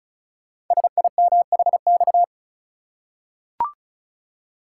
Morse code, heard as a keyed tone, about 36 wpm, 690 Hz. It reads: SIMHX
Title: Code Groups - 5 Letters - 35wpm